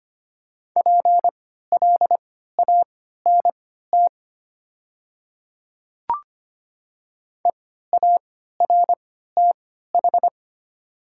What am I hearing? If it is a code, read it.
Morse code, heard as a keyed tone, about 25 wpm, spaced as for 14 wpm, 700 Hz. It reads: PLANT EARTH
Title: 3x Character Spacing - Top 300 Words - 25wpm